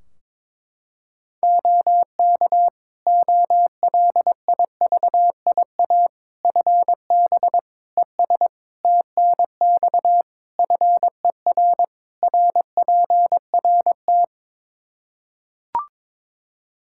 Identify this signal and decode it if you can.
Morse code, heard as a keyed tone, about 22 wpm, 700 Hz. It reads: OK OLIVIA FB ES TNX FER RPRT
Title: Practice QSO - Exchange 2 - Info - 22wpm